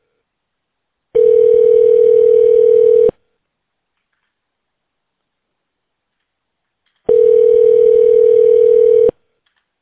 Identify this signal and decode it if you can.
Ringback tone (North America), 440 + 480 Hz.